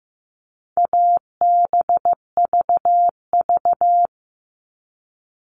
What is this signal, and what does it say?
Morse code, heard as a keyed tone, about 15 wpm, 700 Hz. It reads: ABVV
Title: All Cut Numbers - 4-Digit 15wpm